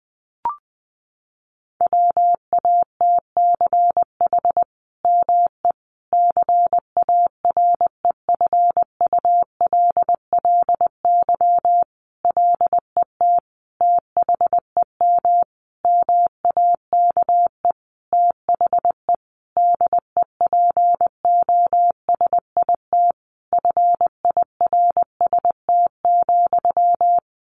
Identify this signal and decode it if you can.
Morse code, heard as a keyed tone, about 19 wpm, 700 Hz. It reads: WATCH ME CAREFULLY LET THEM MAKE THE DEPOSIT FIRST,
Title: Sentences from Top 1300 Words 20wpm